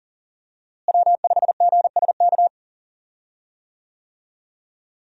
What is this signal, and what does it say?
Morse code, heard as a keyed tone, about 40 wpm, 700 Hz. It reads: W5GSK
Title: USA Call Signs - 5 Characters - Repeated 2x - 40wpm